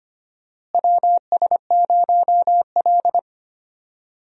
Morse code, heard as a keyed tone, about 25 wpm, 700 Hz. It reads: WS0L
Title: USA Call Signs - 4 Characters - Repeated 3x - 25wpm